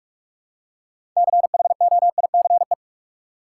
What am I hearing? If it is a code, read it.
Morse code, heard as a keyed tone, about 44 wpm, 700 Hz. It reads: CHOICE
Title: Six Letter Words - 45wpm